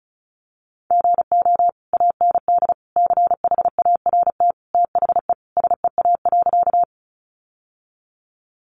Morse code, heard as a keyed tone, about 34 wpm, 700 Hz. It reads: GO AND CHART THE SEA.
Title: Sentences from Top 1300 Words 35wpm